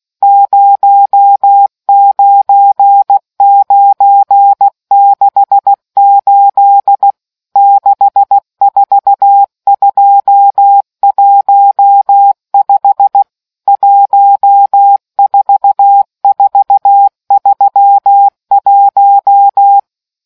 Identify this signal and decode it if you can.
Morse code, heard as a keyed tone, about 16 wpm, 800 Hz. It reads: 09968 64215 14431